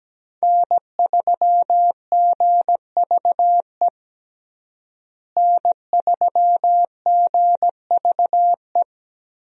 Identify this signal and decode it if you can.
Morse code, heard as a keyed tone, about 17 wpm, 700 Hz. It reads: N3GVE N3GVE
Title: USA Call Signs - 5 Characters - Repeated 3x - 17wpm